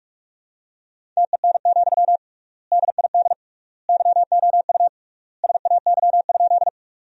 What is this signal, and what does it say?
Morse code, heard as a keyed tone, about 46 wpm, 700 Hz. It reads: TEN, DID YOU SAY?
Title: 3x Word-Spacing - Sentences from Top 400 Words - 45wpm